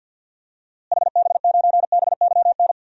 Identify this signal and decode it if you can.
Morse code, heard as a keyed tone, about 47 wpm, 700 Hz. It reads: HB9BYN
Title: DXCC-100 Call Signs - 6 Characters - Repeated 2x - 50wpm